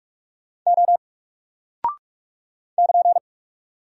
Morse code, heard as a keyed tone, about 44 wpm, 700 Hz. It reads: O <KN>
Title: Single Character (Letter:Number:Punctuation:Prosign) Alt 45wpm